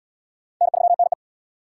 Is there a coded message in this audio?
Yes: Morse code from 0.6 to 1.1 s, heard as a keyed tone, about 72 wpm, 700 Hz. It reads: N3NE